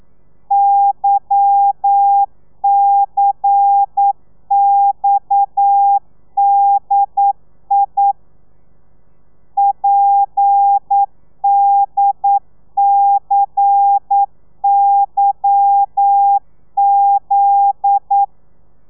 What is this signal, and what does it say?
Morse code, heard as a keyed tone, about 9 wpm, 800 Hz. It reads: YCXDI PDCYZ